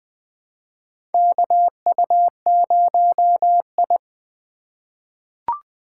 Morse code, heard as a keyed tone, about 20 wpm, 700 Hz. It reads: KU0I
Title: USA Call Signs - 4 Characters - 6x Character Spacing - 20wpm